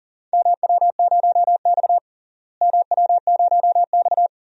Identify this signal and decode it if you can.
Morse code, heard as a keyed tone, about 40 wpm, 700 Hz. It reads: MW0X MW0X
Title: DXCC-100 Call Signs - 4 Characters - Repeated 2x - 40wpm